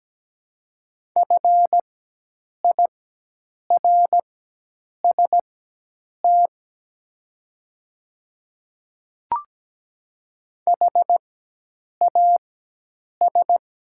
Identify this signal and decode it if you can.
Morse code, heard as a keyed tone, about 17 wpm, spaced as for 8 wpm, 700 Hz. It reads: FIRST HAS
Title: 4x Character Spacing - Top 200 Words - 17wpm